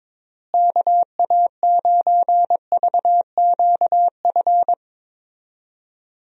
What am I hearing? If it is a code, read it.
Morse code, heard as a keyed tone, about 22 wpm, 700 Hz. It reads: KA9VQF
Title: USA Call Signs - 6 Characters - Repeated 2x - 22wpm